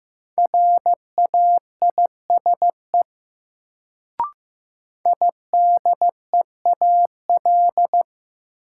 Morse code, heard as a keyed tone, about 16 wpm, 700 Hz. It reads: RAISE IDEAL
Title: Sentences from Top 1400 Words - New Word Drill Alt - 15wpm